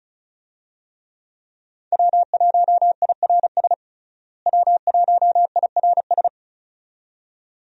Morse code, heard as a keyed tone, about 34 wpm, 700 Hz. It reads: W1IRS W1IRS
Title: USA Call Signs - 5 Characters - Repeated 2x - 35wpm